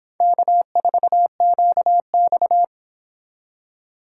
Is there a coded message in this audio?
Yes: Morse code from 0.2 to 2.6 s, heard as a keyed tone, about 26 wpm, 700 Hz. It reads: K4QX